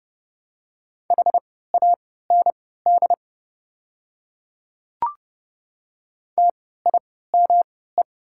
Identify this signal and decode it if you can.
Morse code, heard as a keyed tone, about 31 wpm, spaced as for 17 wpm, 700 Hz. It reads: HAND TIME
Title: 3x Character Spacing - Top 500 Words - 30wpm